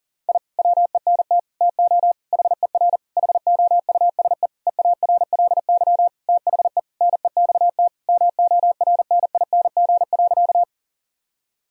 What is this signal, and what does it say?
Morse code, heard as a keyed tone, about 40 wpm, 700 Hz. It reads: I WENT TO HER HOUSE EARLY THE NEXT MORNING.
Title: Sentences from Top 1000 Words - Review Part 2 - 40wpm